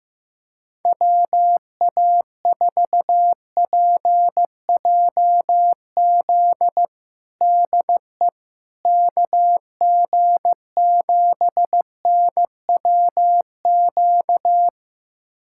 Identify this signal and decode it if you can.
Morse code, heard as a keyed tone, about 16 wpm, 700 Hz. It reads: WA4PJZ DE KG7NWQ
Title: USA Callsign DE Callsign 15wpm